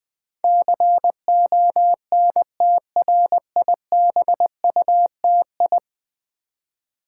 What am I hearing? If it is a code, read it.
Morse code, heard as a keyed tone, about 20 wpm, 700 Hz. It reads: CONTRIBUTI
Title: Build A Word - 5 Letters At A Time - 12-Letter Words - 20wpm